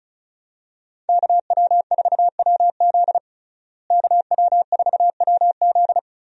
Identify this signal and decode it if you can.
Morse code, heard as a keyed tone, about 36 wpm, 700 Hz. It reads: KW4WZ KW4WZ
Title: USA Call Signs - 5 Characters - Repeated 2x - 35wpm